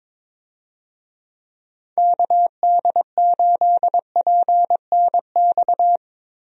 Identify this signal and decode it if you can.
Morse code, heard as a keyed tone, about 22 wpm, 700 Hz. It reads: KD8PNX
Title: USA Call Signs - 6 Characters - Repeated 3x - 22wpm